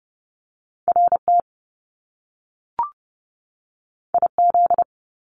Morse code, heard as a keyed tone, about 30 wpm, 700 Hz. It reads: RT IZ
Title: Top Two Letter Combinations in Words 30wpm